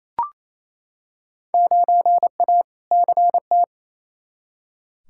Morse code, heard as a keyed tone, about 28 wpm, 700 Hz. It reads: 9A CT
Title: Field Day Exchanges Encore - 28wpm